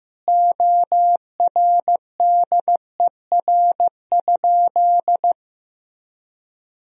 Morse code, heard as a keyed tone, about 15 wpm, 700 Hz. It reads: ORDER?
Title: Sentences from Top 400 Words and Abbreviations 15wpm